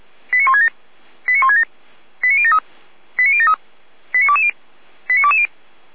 Five-tone selective call (ZVEI), heard as quick runs of five tones, tones of about 70 ms. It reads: 89167 89167 89073 89073 89220 89220